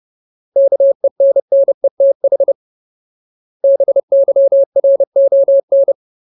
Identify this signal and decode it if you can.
Morse code, heard as a keyed tone, about 30 wpm, 550 Hz. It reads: KENNETH BYRON
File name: Ultimate Word List / 30WPM Ultimate